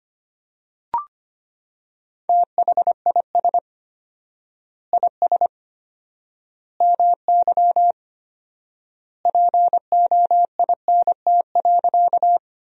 Morse code, heard as a keyed tone, about 25 wpm, 700 Hz. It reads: THIS IS MY POINT.
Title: 4x Word-Spacing - Sentences from Top 200 Words - 25wpm